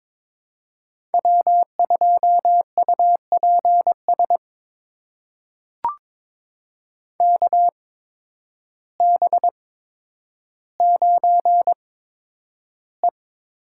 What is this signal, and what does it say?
Morse code, heard as a keyed tone, about 22 wpm, spaced as for 10 wpm, 700 Hz. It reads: W2UPS KB9E
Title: USA Call Signs - 5 Characters - 8x Character Spacing - 22wpm